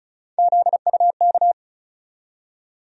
Morse code, heard as a keyed tone, about 36 wpm, 700 Hz. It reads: ZUK
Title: USA Call Sign Postfixes - 3 Letters 35wpm